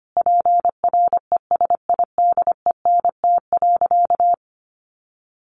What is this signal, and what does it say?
Morse code, heard as a keyed tone, about 25 wpm, 700 Hz. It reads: PRESIDENT.
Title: Sentences from Top 1000 Words 25wpm